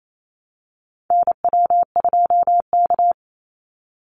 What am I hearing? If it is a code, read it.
Morse code, heard as a keyed tone, about 28 wpm, 690 Hz. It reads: NW2K
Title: Call Signs Repeated 3x - 28wpm